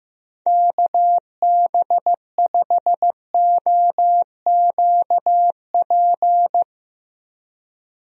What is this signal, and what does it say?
Morse code, heard as a keyed tone, about 15 wpm, 700 Hz. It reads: KB5OQP
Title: USA Call Signs - 6 Characters - 15wpm